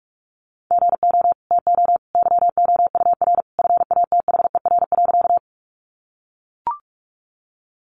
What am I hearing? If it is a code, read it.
Morse code, heard as a keyed tone, about 44 wpm, 700 Hz. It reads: GO TO YOUR FATHER.
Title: Sentences from Top 300 Words 45wpm